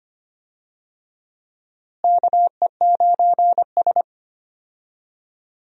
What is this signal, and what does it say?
Morse code, heard as a keyed tone, about 25 wpm, 700 Hz. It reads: KE9S
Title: USA Call Signs - 4 Characters - Repeated 2x - 25wpm